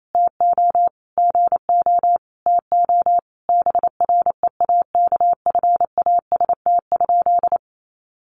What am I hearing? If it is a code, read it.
Morse code, heard as a keyed tone, about 28 wpm, 700 Hz. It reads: TO GO TO BREAKFAST?
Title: Sentences from Top 1500 Words 28wpm